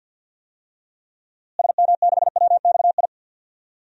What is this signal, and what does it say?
Morse code, heard as a keyed tone, about 52 wpm, 700 Hz. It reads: SM6WXI
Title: DXCC-100 Call Signs - 6 Characters - Repeated 2x - 50wpm